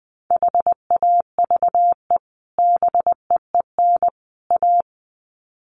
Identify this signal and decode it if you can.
Morse code, heard as a keyed tone, about 20 wpm, 700 Hz. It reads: HAVE BEEN A
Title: Sets of 3 Words 20wpm